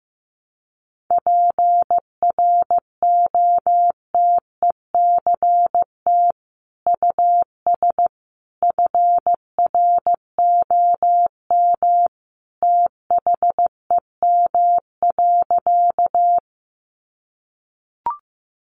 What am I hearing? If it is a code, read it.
Morse code, heard as a keyed tone, about 15 wpm, 700 Hz. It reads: PROTECT US FROM THEM.
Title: Sentences from Top 900 Words 15wpm